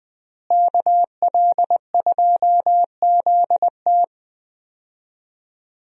Morse code, heard as a keyed tone, about 20 wpm, 700 Hz. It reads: KL2ZT
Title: USA Call Signs - 5 Characters - Repeated 2x - 20wpm